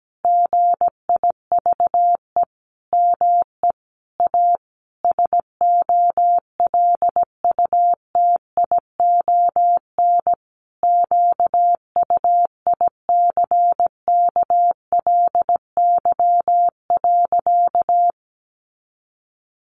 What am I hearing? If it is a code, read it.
Morse code, heard as a keyed tone, about 17 wpm, 700 Hz. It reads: GIVE ME A SOLUTION QUICKLY.
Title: Sentences from Top 1300 Words 17wpm